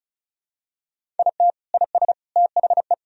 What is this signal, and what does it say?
Morse code, heard as a keyed tone, about 34 wpm, 700 Hz. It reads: IT IS THE
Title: Sentences from Top 1000 Words - Review Part 2 - 35wpm